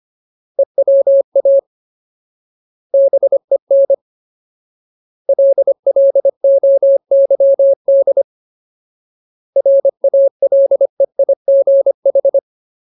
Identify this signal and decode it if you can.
Morse code, heard as a keyed tone, about 25 wpm, 550 Hz. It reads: EWA BEN LLOYD RALEIGH
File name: Ultimate Word List / 25WPM Ultimate